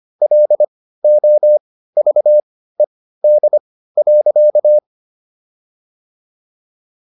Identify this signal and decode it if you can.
Morse code, heard as a keyed tone, about 25 wpm, spaced as for 15 wpm, 600 Hz. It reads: LOVED.